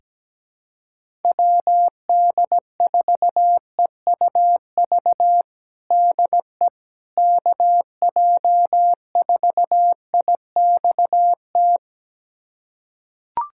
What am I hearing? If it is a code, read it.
Morse code, heard as a keyed tone, about 17 wpm, 700 Hz. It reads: WD4EUV DE KJ4IXT
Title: USA Callsign DE Callsign - 6 Characters - 17wpm